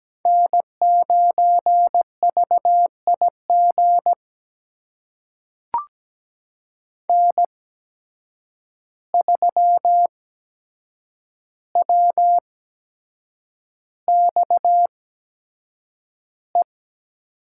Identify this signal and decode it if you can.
Morse code, heard as a keyed tone, about 17 wpm, spaced as for 7 wpm, 700 Hz. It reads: N9VIG N3WXE
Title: USA Call Signs - 5 Characters - 8x Character Spacing - 17wpm